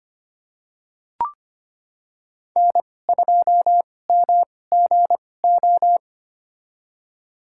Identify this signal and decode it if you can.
Morse code, heard as a keyed tone, about 25 wpm, spaced as for 18 wpm, 700 Hz. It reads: N2MGO
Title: USA Call Signs - 5 Characters - 2x Character Spacing - 25wpm